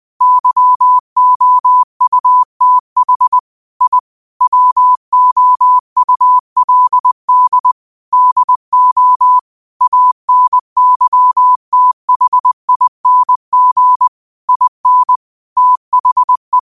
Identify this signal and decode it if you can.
Morse code, heard as a keyed tone, about 20 wpm, 1000 Hz. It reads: YOUTH I WOULD DO ANYTHING IN THE